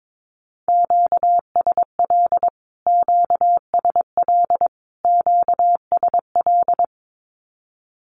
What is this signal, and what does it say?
Morse code, heard as a keyed tone, about 22 wpm, 700 Hz. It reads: QSL QSL QSL
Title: Q-Codes Repeated 3x 22wpm